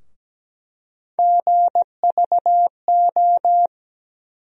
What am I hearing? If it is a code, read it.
Morse code, heard as a keyed tone, about 17 wpm, 700 Hz. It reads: GVO
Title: USA Call Sign Postfixes - 3 Letters 17wpm